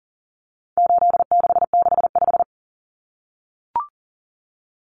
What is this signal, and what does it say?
Morse code, heard as a keyed tone, about 40 wpm, 700 Hz. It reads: 8665
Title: Common Cut Numbers - 4-Digit 40wpm